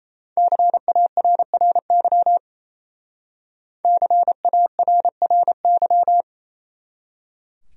Morse code, heard as a keyed tone, about 35 wpm, 700 Hz. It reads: CARRY CARRY